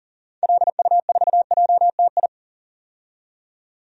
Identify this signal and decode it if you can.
Morse code, heard as a keyed tone, about 40 wpm, 700 Hz. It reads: LU4JTI
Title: DXCC-100 Call Signs - 6 Characters - 40wpm